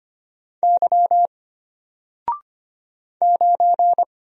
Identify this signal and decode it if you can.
Morse code, heard as a keyed tone, about 25 wpm, 700 Hz. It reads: Y 9
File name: Single-Letter-Number-25wpm